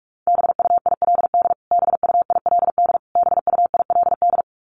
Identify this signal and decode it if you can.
Morse code, heard as a keyed tone, about 44 wpm, 700 Hz. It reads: BUILD BUILD BUILD
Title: Top 500 Words Repeated 45wpm